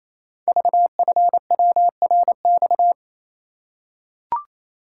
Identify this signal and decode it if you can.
Morse code, heard as a keyed tone, about 28 wpm, 700 Hz. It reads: VFWRX
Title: Code Groups - 5 Letters - 28wpm